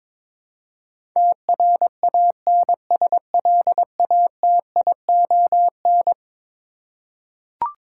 Morse code, heard as a keyed tone, about 22 wpm, 700 Hz. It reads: TRANSLATION
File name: practice / 22wpm / Eleven Letter Words - 22wpm